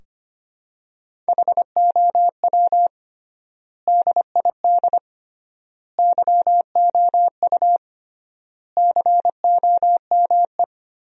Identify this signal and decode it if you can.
Morse code, heard as a keyed tone, about 25 wpm, 700 Hz. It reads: HOW DID YOU COME
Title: 3x Word-Spacing - Sentences from Top 100 Words - 25wpm